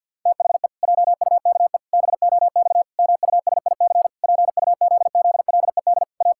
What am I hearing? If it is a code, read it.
Morse code, heard as a keyed tone, about 51 wpm, 700 Hz. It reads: THE JUKE BOX MUSIC PUZZLED A
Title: Pangram Challenge v2 - 50wpm